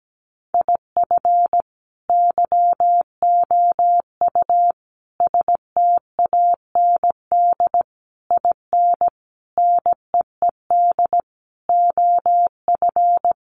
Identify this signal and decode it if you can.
Morse code, heard as a keyed tone, about 17 wpm, 700 Hz. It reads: IF YOU STAND IN NEED OF
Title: Sentences from Top 1200 Words 17wpm